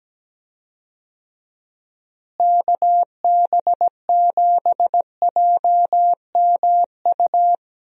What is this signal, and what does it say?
Morse code, heard as a keyed tone, about 17 wpm, 700 Hz. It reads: KB7JMU
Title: USA Call Signs - 6 Characters - Repeated 3x - 17wpm